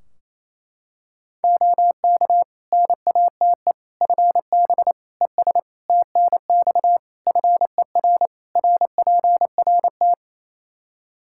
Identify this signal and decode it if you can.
Morse code, heard as a keyed tone, about 28 wpm, 700 Hz. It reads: OK NATE FB ES TNX FER RPRT
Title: Practice QSO - Exchange 2 - Info - 28wpm